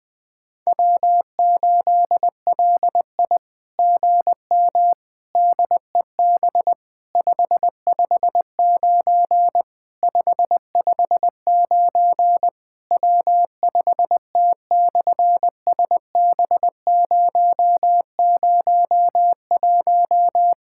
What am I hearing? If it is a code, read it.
Morse code, heard as a keyed tone, about 20 wpm, 700 Hz. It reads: W8LI GM DEB 559 559 W5T/SB001
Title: SOTA QSO Practice 20wpm